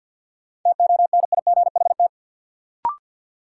Morse code, heard as a keyed tone, about 48 wpm, 700 Hz. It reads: TONIGHT
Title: Sentences from Top 600 Words - New Word Drill - 50wpm